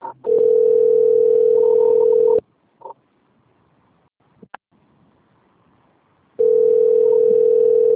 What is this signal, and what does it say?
Ringback tone (North America), 440 + 480 Hz.